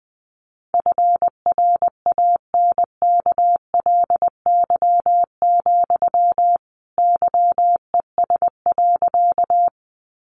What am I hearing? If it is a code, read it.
Morse code, heard as a keyed tone, about 20 wpm, 700 Hz. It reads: FRANKLY, YES.
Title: A Sound Of Thunder 20wpm